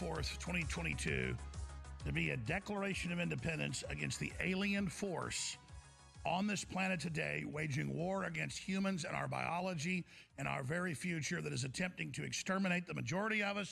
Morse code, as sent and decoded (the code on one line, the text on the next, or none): none